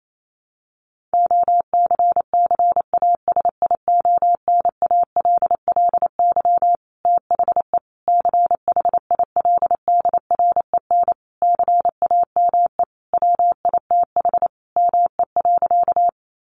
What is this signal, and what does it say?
Morse code, heard as a keyed tone, about 28 wpm, 700 Hz. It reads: OCCASIONALLY THE CHILDREN CAME WITH ME.
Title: Sentences from Top 1300 Words 28wpm